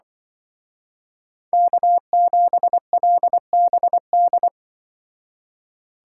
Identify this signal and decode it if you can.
Morse code, heard as a keyed tone, about 23 wpm, 700 Hz. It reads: K7LBD
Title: Speed Racing - USA 1x3 Call Signs 20wpm